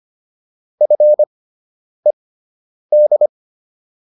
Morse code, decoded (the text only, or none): FED